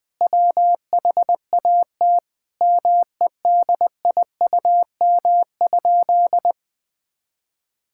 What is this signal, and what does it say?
Morse code, heard as a keyed tone, about 20 wpm, 700 Hz. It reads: WHAT MEDIUM?
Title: Sentences from Top 1000 Words - Review Part 6 - 20wpm